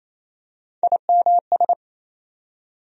Morse code, heard as a keyed tone, about 28 wpm, 700 Hz. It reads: IMS
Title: USA Call Sign Postfixes - 3 Letters 28wpm